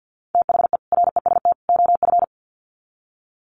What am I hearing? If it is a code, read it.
Morse code, heard as a keyed tone, about 50 wpm, 700 Hz. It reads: THE REST OF
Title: Sets of 3 Words 50wpm